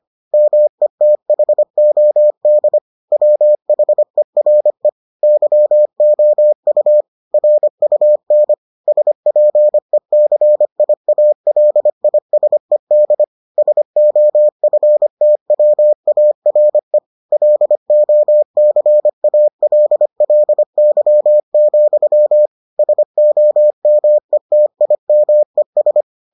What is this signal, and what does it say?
Morse code, heard as a keyed tone, about 25 wpm, 600 Hz. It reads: METHOD WHERE YOU RUN SPECIALISED SOFTWARE LOCALLY, SOMETIMES